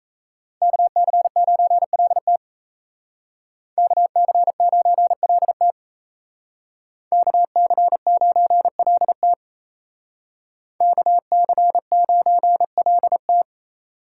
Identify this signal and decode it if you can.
Morse code, heard as a keyed tone, about 36 wpm, 700 Hz. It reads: KC9LT KC9LT KC9LT KC9LT